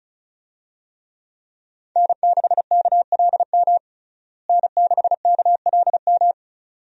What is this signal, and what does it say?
Morse code, heard as a keyed tone, about 36 wpm, 700 Hz. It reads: N6KLM N6KLM
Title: USA Call Signs - 5 Characters - Repeated 2x - 35wpm